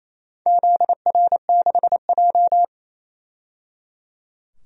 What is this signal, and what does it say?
Morse code, heard as a keyed tone, about 28 wpm, 700 Hz. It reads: ZR6J
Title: DXCC-100 Call Signs - 4 Characters - 28wpm